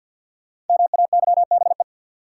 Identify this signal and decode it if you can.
Morse code, heard as a keyed tone, about 51 wpm, 700 Hz. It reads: MAYBE
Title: Sentences from Top 1000 Words - New Word Drill - 50wpm